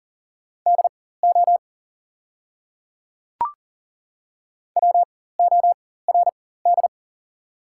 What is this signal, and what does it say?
Morse code, heard as a keyed tone, about 40 wpm, spaced as for 19 wpm, 690 Hz. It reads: DO WORD
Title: 4x Character Spacing - Top 100 Words - 40wpm